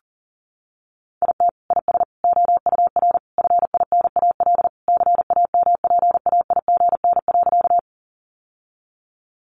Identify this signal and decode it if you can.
Morse code, heard as a keyed tone, about 40 wpm, 700 Hz. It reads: IT IS OUR FINAL CAMPAIGN.
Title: Sentences from Top 1200 Words 40wpm